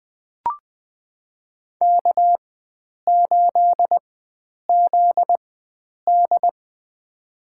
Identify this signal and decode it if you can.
Morse code, heard as a keyed tone, about 20 wpm, spaced as for 9 wpm, 700 Hz. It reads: K8ZD
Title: USA Call Signs - 4 Characters - 4x Character Spacing - 20wpm